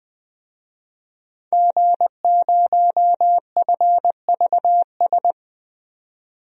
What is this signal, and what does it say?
Morse code, heard as a keyed tone, about 20 wpm, 700 Hz. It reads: G0FVS
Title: DXCC-100 Call Signs - 5 Characters - 20wpm